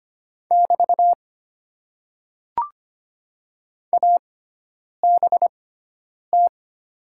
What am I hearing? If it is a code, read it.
Morse code, heard as a keyed tone, about 25 wpm, spaced as for 9 wpm, 700 Hz. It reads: <BT> ABT
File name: Top 90 QSO Elements - 6x Character Spacing - 25wpm